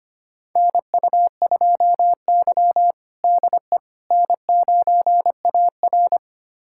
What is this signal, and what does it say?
Morse code, heard as a keyed tone, about 25 wpm, 700 Hz. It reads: NU2Y DE N9AR
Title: USA Callsign DE Callsign - 4 Characters - 25wpm